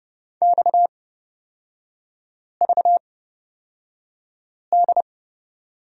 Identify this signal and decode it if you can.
Morse code, heard as a keyed tone, about 30 wpm, spaced as for 5 wpm, 690 Hz. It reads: XVD
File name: Lesson 37e - Letter x - Three Characters 30wpm